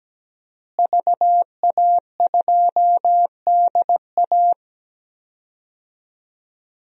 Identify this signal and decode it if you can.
Morse code, heard as a keyed tone, about 17 wpm, 700 Hz. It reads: VA2DA
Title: DXCC-100 Call Signs - 5 Characters - Repeated 3x - 17wpm